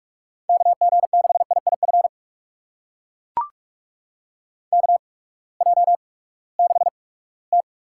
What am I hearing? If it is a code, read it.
Morse code, heard as a keyed tone, about 41 wpm, spaced as for 22 wpm, 700 Hz. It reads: KG6IIF KJ6T
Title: USA Call Signs - 6 Characters - 8x Character Spacing - 45wpm